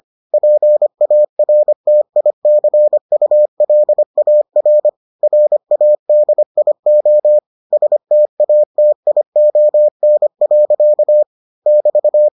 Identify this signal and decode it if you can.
Morse code, heard as a keyed tone, about 25 wpm, 600 Hz. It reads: PARTICULAR RADIO STATION. <BT>